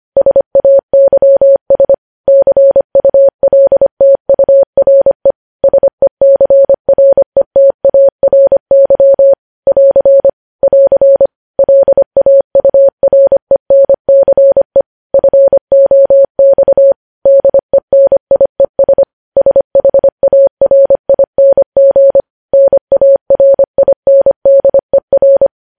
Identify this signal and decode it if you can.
Morse code, heard as a keyed tone, about 25 wpm, 550 Hz. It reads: SAYS CULTURE SECRETARY <AR> <AR> LAURENCE FOX DENIES SHARING NARINDER
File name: podcast / News headlines-1x-25x25